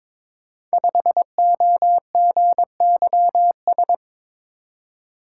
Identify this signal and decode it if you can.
Morse code, heard as a keyed tone, about 22 wpm, 690 Hz. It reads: 5OGYS
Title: Code Groups - 5 Letter:Numbers - 22wpm